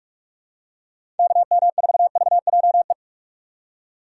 Morse code, heard as a keyed tone, about 43 wpm, 700 Hz. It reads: KM4VJE
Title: USA Call Signs - 6 Characters - 2x Character Spacing - 45wpm